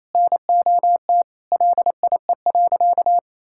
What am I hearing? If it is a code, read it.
Morse code, heard as a keyed tone, about 28 wpm, 700 Hz. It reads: NOT LIE.
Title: Sentences from Top 1600 Words 28wpm